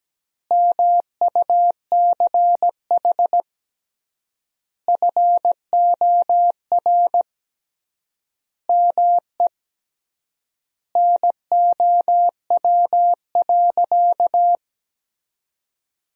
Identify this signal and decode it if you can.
Morse code, heard as a keyed tone, about 17 wpm, 700 Hz. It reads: MUCH FOR ME NOW.
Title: 3x Word-Spacing - Sentences from Top 200 Words - 17wpm